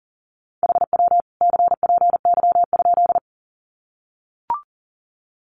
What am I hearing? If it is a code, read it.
Morse code, heard as a keyed tone, about 40 wpm, 700 Hz. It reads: HW CPY?
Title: Common QSO Phrases 40wpm